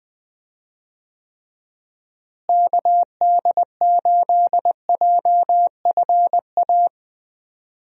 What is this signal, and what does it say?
Morse code, heard as a keyed tone, about 20 wpm, 700 Hz. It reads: KD8JFA
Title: USA Call Signs - 6 Characters - Repeated 3x - 20wpm